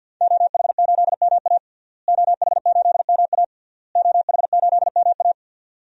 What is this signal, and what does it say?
Morse code, heard as a keyed tone, about 52 wpm, 700 Hz. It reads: OH8MA OH8MA OH8MA
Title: DXCC-100 Call Signs - 5 Characters - Repeated 3x - 50wpm